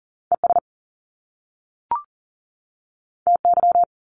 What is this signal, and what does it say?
Morse code, heard as a keyed tone, about 40 wpm, 700 Hz. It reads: ES TY
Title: Top Two Letter Combinations in Words 40wpm